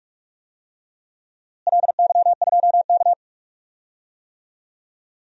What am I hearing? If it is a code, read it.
Morse code, heard as a keyed tone, about 42 wpm, 700 Hz. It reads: LY2K
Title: DXCC-100 Call Signs - 4 Characters - Repeated 3x - 45wpm